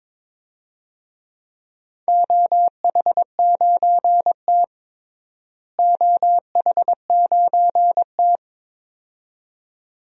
Morse code, heard as a keyed tone, about 21 wpm, 700 Hz. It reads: OH9T OH9T
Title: DXCC-100 Call Signs - 4 Characters - Repeated 2x - 22wpm